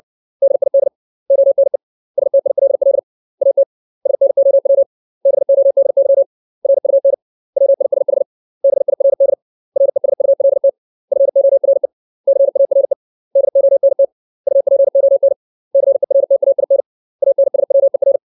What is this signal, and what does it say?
Morse code, heard as a keyed tone, about 60 wpm, 550 Hz. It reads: DID ONE STILL AT SNOW BODY RUN WISH BIRD RESULT FORE CARE DONT UPON CERTAIN ANSWER
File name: 632 words / 632 Words 60WPM